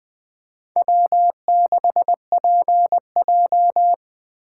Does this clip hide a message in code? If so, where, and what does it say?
Yes: Morse code from 0.8 to 3.9 s, heard as a keyed tone, about 20 wpm, 700 Hz. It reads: W6PJ